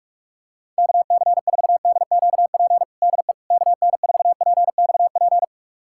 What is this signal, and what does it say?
Morse code, heard as a keyed tone, about 45 wpm, 700 Hz. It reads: KC4DQP DE KN4PXP
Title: USA Callsign DE Callsign 45wpm